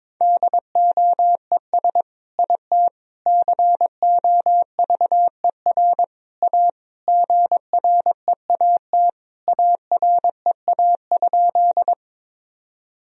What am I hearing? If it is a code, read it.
Morse code, heard as a keyed tone, about 21 wpm, 700 Hz. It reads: DOES IT COVER A GREAT AREA?
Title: Sentences from Top 1000 Words - Review Part 2 - 22wpm